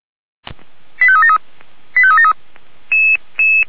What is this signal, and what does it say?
Five-tone selective call (ZVEI), heard as quick runs of five tones, tones of about 70 ms. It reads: 85373 85373
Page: SDR/ZVEI